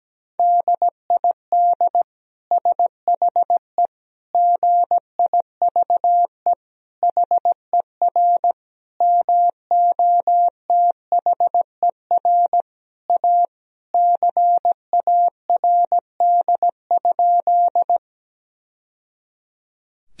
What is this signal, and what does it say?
Morse code, heard as a keyed tone, about 17 wpm, 700 Hz. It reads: DID SHE GIVE HER MOTHER A CARD?
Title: Sentences from Top 1000 Words - Review Part 4 - 17wpm